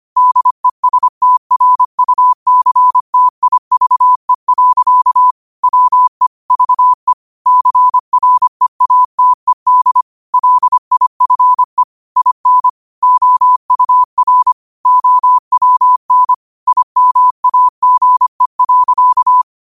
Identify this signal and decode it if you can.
Morse code, heard as a keyed tone, about 25 wpm, 1000 Hz. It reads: DESTRUCTIVE. WE VE CREATED LIFE IN OUR OWN IMAGE.
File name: qotdfiles / 25wpm